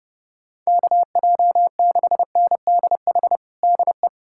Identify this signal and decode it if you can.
Morse code, heard as a keyed tone, about 30 wpm, 700 Hz. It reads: KJ6NDH DE
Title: USA Callsign DE Callsign 30wpm